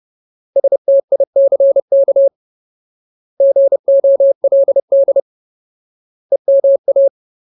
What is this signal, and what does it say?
Morse code, heard as a keyed tone, about 30 wpm, 550 Hz. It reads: STICK GOLD EMA
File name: Ultimate Word List / 30WPM Ultimate